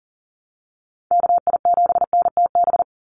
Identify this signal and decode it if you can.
Morse code, heard as a keyed tone, about 40 wpm, 700 Hz. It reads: KI7NTB
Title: USA Call Signs 40wpm